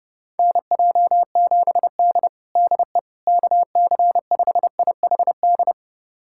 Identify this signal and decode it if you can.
Morse code, heard as a keyed tone, about 30 wpm, 700 Hz. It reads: NJ7D DE KC5IHD
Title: USA Callsign DE Callsign 30wpm